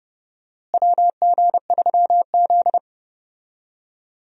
Morse code, heard as a keyed tone, about 30 wpm, 700 Hz. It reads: WG3Z
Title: USA Call Signs - 4 Characters - Repeated 2x - 30wpm